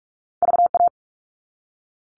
Morse code, heard as a keyed tone, about 46 wpm, 700 Hz. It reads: VA
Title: All Cut Numbers - 2-Digit 45wpm